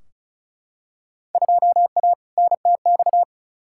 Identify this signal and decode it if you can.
Morse code, heard as a keyed tone, about 36 wpm, 700 Hz. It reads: 2A NTX
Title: Field Day Exchanges - 35wpm